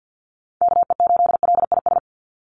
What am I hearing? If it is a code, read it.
Morse code, heard as a keyed tone, about 48 wpm, 700 Hz. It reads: KE8LIS
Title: USA Call Signs 50wpm